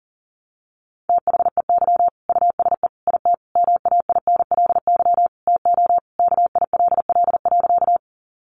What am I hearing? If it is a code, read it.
Morse code, heard as a keyed tone, about 40 wpm, 700 Hz. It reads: THEY USE IT MAINLY TO KILL.
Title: Sentences from Top 1400 Words 40wpm